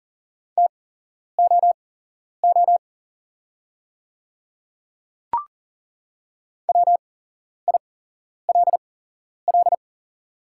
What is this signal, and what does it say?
Morse code, heard as a keyed tone, about 39 wpm, spaced as for 11 wpm, 690 Hz. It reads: TOO WILL
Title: 8x Character Spacing - Top 300 Words - 40wpm